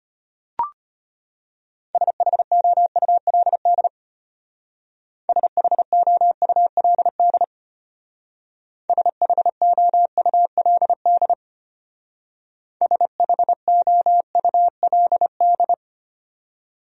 Morse code, heard as a keyed tone, about 32 wpm, 700 Hz. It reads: SHOULD SHOULD SHOULD SHOULD